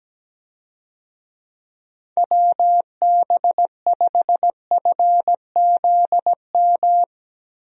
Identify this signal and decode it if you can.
Morse code, heard as a keyed tone, about 17 wpm, 700 Hz. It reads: WB5FZM
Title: USA Call Signs - 6 Characters - Repeated 3x - 17wpm